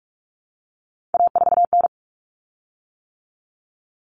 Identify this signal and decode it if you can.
Morse code, heard as a keyed tone, about 44 wpm, 700 Hz. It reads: A4N
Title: Common Cut Numbers - 3-Digit 45wpm